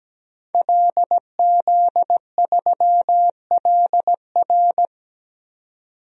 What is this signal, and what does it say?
Morse code, heard as a keyed tone, about 17 wpm, 700 Hz. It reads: LZ3LR
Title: DXCC-100 Call Signs - 5 Characters - 17wpm